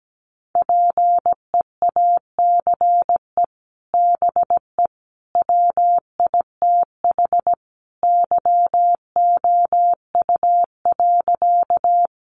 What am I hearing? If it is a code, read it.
Morse code, heard as a keyed tone, about 17 wpm, 700 Hz. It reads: PEACE BE WITH YOU.
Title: Sentences from Top 1500 Words 17wpm